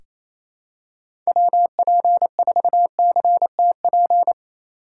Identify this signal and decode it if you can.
Morse code, heard as a keyed tone, about 28 wpm, 700 Hz. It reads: WP4CTP
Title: USA Call Signs - 6 Characters - 2x Character Spacing - 28wpm